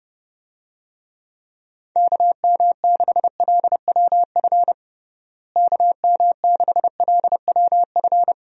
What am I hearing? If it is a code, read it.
Morse code, heard as a keyed tone, about 30 wpm, 700 Hz. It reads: KM6LWF KM6LWF
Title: USA Call Signs - 6 Characters - Repeated 2x - 30wpm